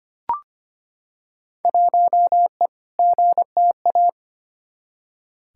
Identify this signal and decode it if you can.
Morse code, heard as a keyed tone, about 25 wpm, 700 Hz. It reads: 1E GTA
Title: Field Day Exchanges - 25wpm